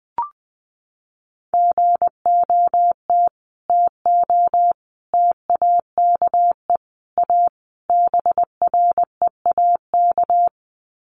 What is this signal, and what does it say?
Morse code, heard as a keyed tone, about 20 wpm, 700 Hz. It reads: GOT TO TAKE A BREAK
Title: Sets of 5 Words Encore 20wpm